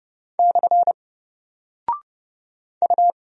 Morse code, heard as a keyed tone, about 30 wpm, 700 Hz. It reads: / U
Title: Single Character (Letter:Number:Punctuation:Prosign) 30wpm